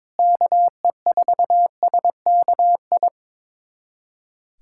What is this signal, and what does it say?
Morse code, heard as a keyed tone, about 22 wpm, 700 Hz. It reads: KE4SKI
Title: USA Callsign DE Callsign - 6 Characters - 22wpm